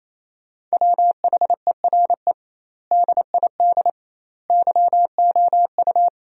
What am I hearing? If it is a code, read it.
Morse code, heard as a keyed tone, about 28 wpm, 700 Hz. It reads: WHERE DID YOU
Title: 2x Word-Spacing - Sentences from Top 200 Words - 28wpm